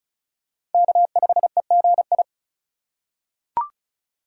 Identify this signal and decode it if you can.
Morse code, heard as a keyed tone, about 36 wpm, 700 Hz. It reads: K5EGI
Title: USA Call Signs - 4x Character Spacing - 35wpm